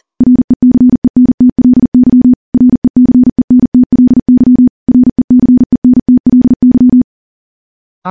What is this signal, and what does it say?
Morse code, heard as a keyed tone, about 40 wpm, 250 Hz. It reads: RECENTLY RECENTLY RECENTLY
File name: morse / morse-training-40WPM